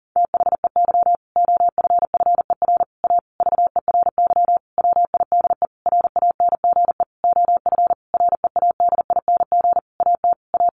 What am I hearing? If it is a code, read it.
Morse code, heard as a keyed tone, about 40 wpm, 700 Hz. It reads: THEY OFFER A VERY WIDE RANGE OF READING AT A
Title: Sentences from Top 900 Words 40wpm